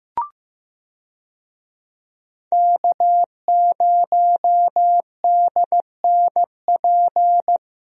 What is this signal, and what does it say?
Morse code, heard as a keyed tone, about 15 wpm, 700 Hz. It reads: K0DNP
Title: USA Call Signs - 5 Characters - Repeated 2x - 15wpm